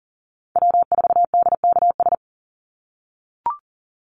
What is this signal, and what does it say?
Morse code, heard as a keyed tone, about 40 wpm, 700 Hz. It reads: W4DKS
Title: Call Signs - 40wpm